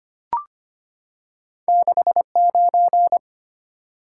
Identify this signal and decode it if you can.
Morse code, heard as a keyed tone, about 25 wpm, 690 Hz. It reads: 69
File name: Two Letter-Number Combinations - 25wpm